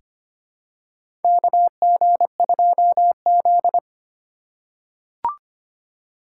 Morse code, heard as a keyed tone, about 25 wpm, 700 Hz. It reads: KG2Z
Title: USA Call Signs - 4 Characters - 2x Character Spacing - 25wpm